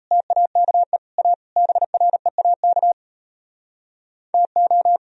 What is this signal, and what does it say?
Morse code, heard as a keyed tone, about 36 wpm, 690 Hz. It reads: TAKE A BREAK TO